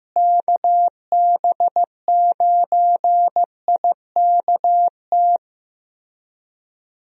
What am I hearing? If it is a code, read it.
Morse code, heard as a keyed tone, about 15 wpm, 700 Hz. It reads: KB9IKT
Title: USA Call Signs - 6 Characters - Repeated 3x - 15wpm